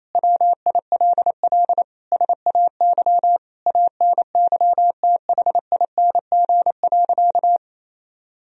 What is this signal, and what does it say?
Morse code, heard as a keyed tone, about 27 wpm, 700 Hz. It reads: WILL SAY ANYTHING.
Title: Sentences from Top 1000 Words - Review Part 2 - 28wpm